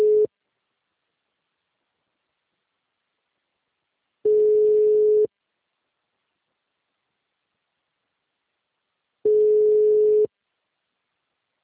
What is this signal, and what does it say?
Ringback tone (Europe, 425 Hz).
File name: out- -2007